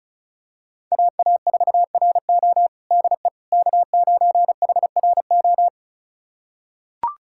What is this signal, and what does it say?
Morse code, heard as a keyed tone, about 36 wpm, 690 Hz. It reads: AA4RO DE K9HRO
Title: USA Callsign DE Callsign - 5 Characters - 35wpm